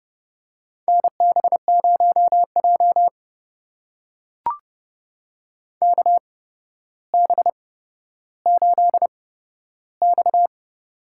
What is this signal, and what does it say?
Morse code, heard as a keyed tone, about 32 wpm, spaced as for 13 wpm, 700 Hz. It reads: NB0J KB8X
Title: USA Call Signs - 8x Character Spacing - 30wpm